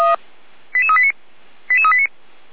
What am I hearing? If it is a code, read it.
Five-tone selective call (ZVEI), heard as quick runs of five tones, tones of about 70 ms. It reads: 88389 88389